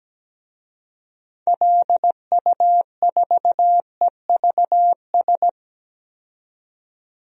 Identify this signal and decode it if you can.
Morse code, heard as a keyed tone, about 17 wpm, 700 Hz. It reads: LU4EVS
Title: DXCC-100 Call Signs - 6 Characters - 17wpm